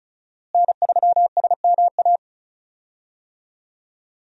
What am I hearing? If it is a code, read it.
Morse code, heard as a keyed tone, about 34 wpm, 700 Hz. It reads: N3SMA
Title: USA Call Signs - 5 Characters - Repeated 2x - 35wpm